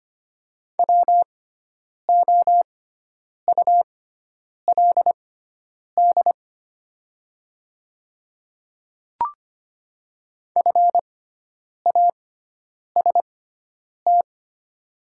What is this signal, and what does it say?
Morse code, heard as a keyed tone, about 25 wpm, spaced as for 9 wpm, 700 Hz. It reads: WOULD FAST
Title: 6x Character Spacing - Top 500 Words - 25wpm